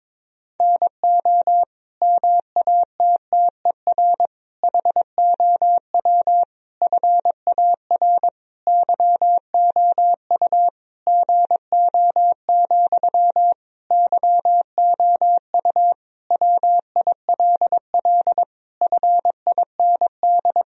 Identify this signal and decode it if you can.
Morse code, heard as a keyed tone, about 22 wpm, 700 Hz. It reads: NO MATTER HOW FAR YOU GO, YOU WILL FIND
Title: Sentences from Top 1000 Words - Review Part 1 - 22wpm